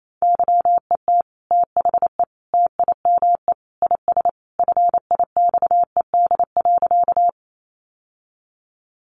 Morse code, heard as a keyed tone, about 28 wpm, 700 Hz. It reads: YET THE TIME IS FIXED.
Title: Sentences from Top 1100 Words 28wpm